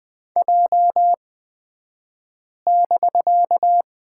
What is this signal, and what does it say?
Morse code, heard as a keyed tone, about 20 wpm, spaced as for 5 wpm, 700 Hz. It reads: J<BK>
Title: 249 Lesson 40d - BK - Two Characters 20wpm